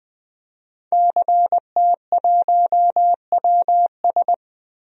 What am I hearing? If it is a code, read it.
Morse code, heard as a keyed tone, about 20 wpm, 700 Hz. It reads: CT1WS
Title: DXCC-100 Call Signs - 5 Characters - Repeated 2x - 20wpm